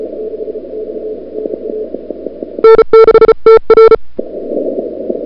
Morse code, heard as a keyed tone, about 36 wpm, 1340 Hz. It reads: N6TR
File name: CWSprint / February 2007 CW Sprint / 01-2007 CW Sprint Hour1